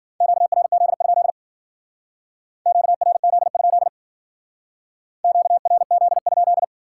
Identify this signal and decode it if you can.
Morse code, heard as a keyed tone, about 52 wpm, 700 Hz. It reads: QRZ? QRZ? QRZ?